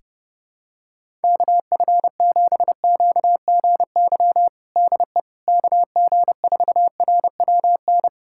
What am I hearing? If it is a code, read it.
Morse code, heard as a keyed tone, about 30 wpm, 700 Hz. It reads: KF7QGY DE KG4RWN
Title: USA Callsign DE Callsign - 6 Characters - 30wpm